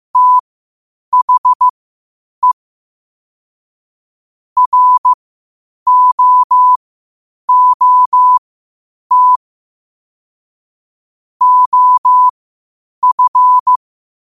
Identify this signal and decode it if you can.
Morse code, heard as a keyed tone, about 15 wpm, spaced as for 8 wpm, 1000 Hz. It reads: THE ROOT OF
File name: qotdfiles / 8wpm